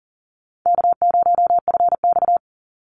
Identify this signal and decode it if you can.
Morse code, heard as a keyed tone, about 40 wpm, 700 Hz. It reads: K0FX
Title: Call Signs - 40wpm